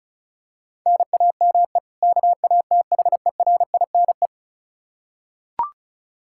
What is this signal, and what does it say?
Morse code, heard as a keyed tone, about 36 wpm, 700 Hz. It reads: NAME KATHERINE
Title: Practice QSO - Exchange 1 - Beginning - 35wpm